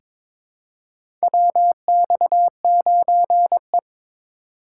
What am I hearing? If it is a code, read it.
Morse code, heard as a keyed tone, about 22 wpm, 700 Hz. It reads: WX9E
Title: USA Call Signs - 4 Characters - 2x Character Spacing - 22wpm